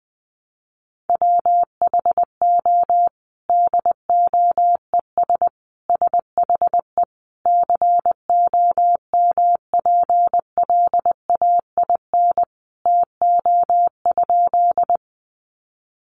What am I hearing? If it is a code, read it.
Morse code, heard as a keyed tone, about 20 wpm, 700 Hz. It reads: WHO DOES SHE COMPLAIN TO?
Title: Sentences from Top 1300 Words 20wpm